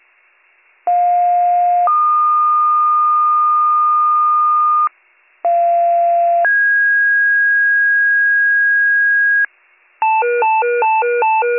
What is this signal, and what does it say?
Two-tone page, heard as a one-second tone, then a three-second tone, about 710 then 1220 Hz, about 690 then 1650 Hz, with noise. It is heard from 0.9 to 9.5 s.